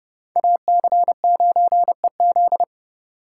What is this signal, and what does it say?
Morse code, heard as a keyed tone, about 38 wpm, 700 Hz. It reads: AC9EZ